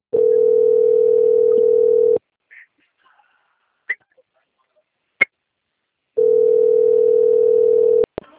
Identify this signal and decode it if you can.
Ringback tone (North America), 440 + 480 Hz.